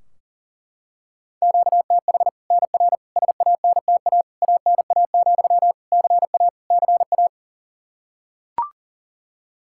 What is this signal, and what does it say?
Morse code, heard as a keyed tone, about 40 wpm, 700 Hz. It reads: QTH NR SANTA ANA, CA CA
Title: Practice QSO - Exchange 1 - Beginning - 40wpm